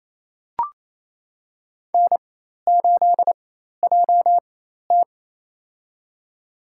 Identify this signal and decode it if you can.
Morse code, heard as a keyed tone, about 28 wpm, spaced as for 13 wpm, 700 Hz. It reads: N8JT
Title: USA Call Signs - 4 Characters - 4x Character Spacing - 28wpm